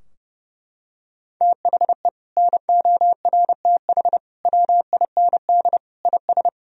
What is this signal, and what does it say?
Morse code, heard as a keyed tone, about 30 wpm, 700 Hz. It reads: THE NORTH WIND IS
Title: Sentences from Top 1000 Words - Review Part 2 - 30wpm